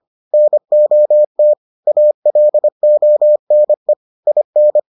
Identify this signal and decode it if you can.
Morse code, heard as a keyed tone, about 25 wpm, 600 Hz. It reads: NOT ALONE IN